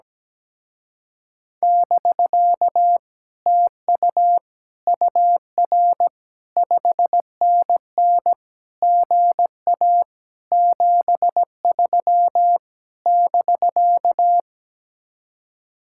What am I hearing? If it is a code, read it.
Morse code, heard as a keyed tone, about 17 wpm, 700 Hz. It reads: <BK> TU UR 5NN GA 73 <BK>
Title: Speed-Racing Common QSO Phrases - 17wpm